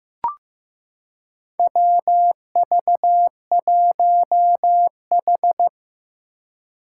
Morse code, heard as a keyed tone, about 15 wpm, 700 Hz. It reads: WV1H
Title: Four Challenging Letter-Number Combinations 15wpm